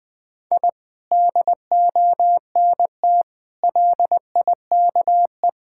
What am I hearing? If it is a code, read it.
Morse code, heard as a keyed tone, about 20 wpm, 700 Hz. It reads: I DONT LIKE
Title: Sentences from Top 1000 Words - Review Part 6 - 20wpm